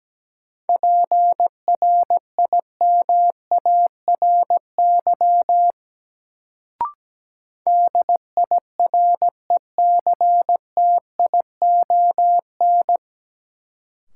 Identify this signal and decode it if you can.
Morse code, heard as a keyed tone, about 17 wpm, 700 Hz. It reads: PRIMARY DIRECTION
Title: Sentences from Top 1000 Words - Review Part 5 - New Word Drill - 17wpm